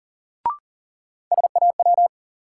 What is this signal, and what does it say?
Morse code, heard as a keyed tone, about 40 wpm, 700 Hz. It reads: SAW
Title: Sentences from Top 300 Words - New Word Drill - 40wpm